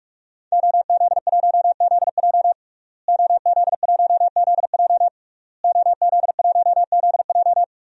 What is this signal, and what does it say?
Morse code, heard as a keyed tone, about 46 wpm, 700 Hz. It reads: OZ1ZJ OZ1ZJ OZ1ZJ
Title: DXCC-100 Call Signs - 5 Characters - Repeated 3x - 45wpm